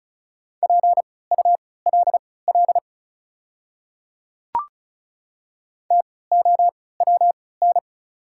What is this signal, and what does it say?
Morse code, heard as a keyed tone, about 36 wpm, spaced as for 20 wpm, 690 Hz. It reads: PULL TOWN
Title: 3x Character Spacing - Top 500 Words - 35wpm